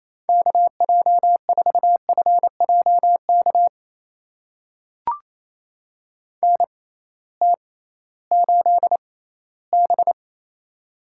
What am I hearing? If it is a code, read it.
Morse code, heard as a keyed tone, about 28 wpm, spaced as for 16 wpm, 700 Hz. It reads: KJ4FJK NT8B
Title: USA Call Signs - 6x Character Spacing - 28wpm